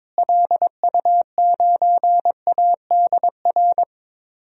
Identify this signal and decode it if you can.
Morse code, heard as a keyed tone, about 22 wpm, 700 Hz. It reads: LU9ADR
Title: DXCC-100 Call Signs - 6 Characters - Repeated 2x - 22wpm